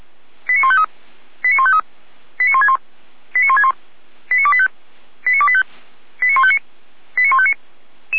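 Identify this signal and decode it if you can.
Five-tone selective call (ZVEI), heard as quick runs of five tones, tones of about 70 ms. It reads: 89153 89153 89161 89161 89285 89285 89159 89159
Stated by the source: ZVEI five-tone selective call 89150 received by SDR.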